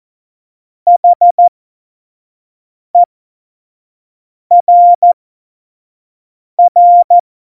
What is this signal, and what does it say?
Morse code, heard as a keyed tone, about 14 wpm, spaced as for 5 wpm, 700 Hz. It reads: HERR
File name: Morsekurs / Witze / Witz 20. August 2019, 25 F70